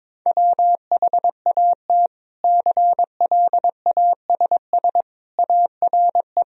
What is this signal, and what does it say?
Morse code, heard as a keyed tone, about 22 wpm, 700 Hz. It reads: WHAT CLASS ARE
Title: Sentences from Top 1000 Words - Review Part 2 - 22wpm